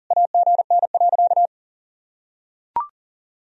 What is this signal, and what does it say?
Morse code, heard as a keyed tone, about 40 wpm, 700 Hz. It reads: AGN.
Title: Sentences from Top 500 Words and Abbreviations 40wpm